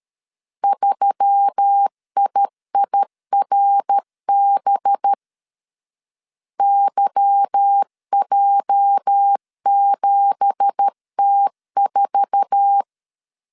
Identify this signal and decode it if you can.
Morse code, heard as a keyed tone, about 13 wpm, 800 Hz. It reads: 3IIRB YJ7T4
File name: CW / DET / 1 9 1500C S B 12wpm 3819